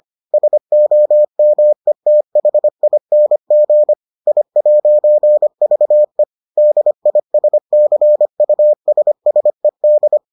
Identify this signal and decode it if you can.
Morse code, heard as a keyed tone, about 25 wpm, 600 Hz. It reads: SOMETHING I'VE DISCUSSED